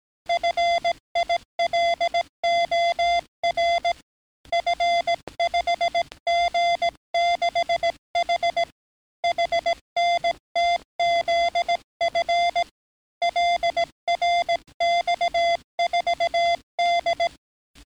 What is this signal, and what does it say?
Morse code, heard as a keyed tone, about 17 wpm, 690 Hz. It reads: FILOR F5G6H HNTZF LRX4D
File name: training / NAVPERS 11049to11060 / NAVPERS 11057 RA RB